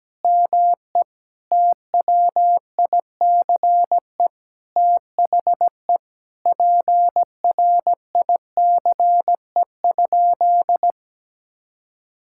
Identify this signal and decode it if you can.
Morse code, heard as a keyed tone, about 17 wpm, 700 Hz. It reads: ME TWICE THE PRICE?
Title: Sentences from Top 1600 Words 17wpm